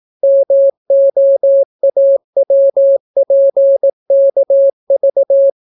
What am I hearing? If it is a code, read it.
Morse code, heard as a keyed tone, about 18 wpm, 550 Hz. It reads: MOAWPKV